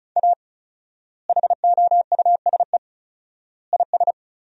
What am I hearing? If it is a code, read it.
Morse code, heard as a keyed tone, about 34 wpm, 700 Hz. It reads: A HOUSE IS
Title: 4x Word-Spacing - Sentences from Top 400 Words - 35wpm